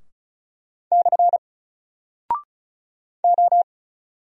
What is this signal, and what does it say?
Morse code, heard as a keyed tone, about 35 wpm, 700 Hz. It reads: / O
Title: Single Character (Letter:Number:Punctuation:Prosign) 35wpm